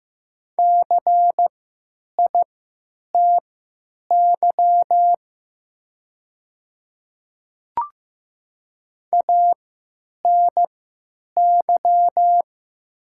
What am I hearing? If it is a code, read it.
Morse code, heard as a keyed tone, about 15 wpm, spaced as for 9 wpm, 700 Hz. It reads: CITY ANY
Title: 3x Character Spacing - Top 300 Words - 15wpm